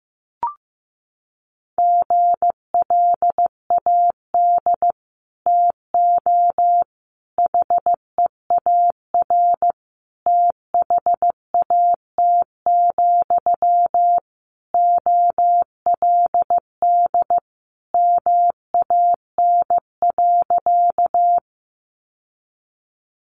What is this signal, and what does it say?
Morse code, heard as a keyed tone, about 15 wpm, 700 Hz. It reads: GLAD TO HEAR THAT, OLD MAN.
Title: Sentences from Top 800 Words 15wpm